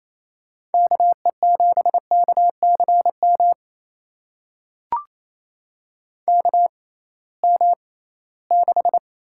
Morse code, heard as a keyed tone, about 28 wpm, spaced as for 18 wpm, 700 Hz. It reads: KE7KCM KM6
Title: USA Call Signs - 6 Characters - 6x Character Spacing - 28wpm